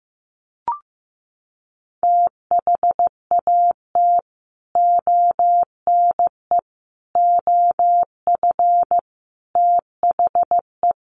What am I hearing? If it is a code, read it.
Morse code, heard as a keyed tone, about 15 wpm, 700 Hz. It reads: THAT ONE OF THE
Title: Sets of 4 Words 15wpm